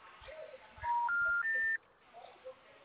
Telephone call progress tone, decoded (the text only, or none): special information tone